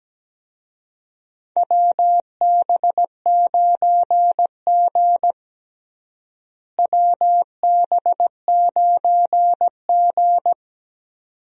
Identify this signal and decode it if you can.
Morse code, heard as a keyed tone, about 17 wpm, 700 Hz. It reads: WB9G WB9G
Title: USA Call Signs - 4 Characters - Repeated 2x - 17wpm